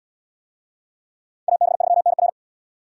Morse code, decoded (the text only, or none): AB3NR